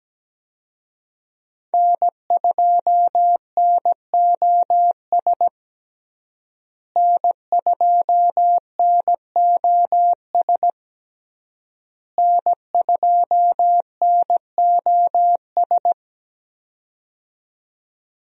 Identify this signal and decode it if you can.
Morse code, heard as a keyed tone, about 17 wpm, 700 Hz. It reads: N2NOS N2NOS N2NOS
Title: USA Call Signs - 5 Characters - Repeated 3x - 17wpm